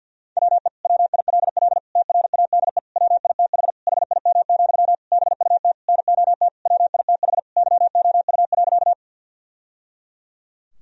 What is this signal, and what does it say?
Morse code, heard as a keyed tone, about 50 wpm, 700 Hz. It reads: WE WILL TRADE WITH HIM, BUT NOT WITH YOU.
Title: Sentences from Top 1000 Words - Review Part 4 - 50wpm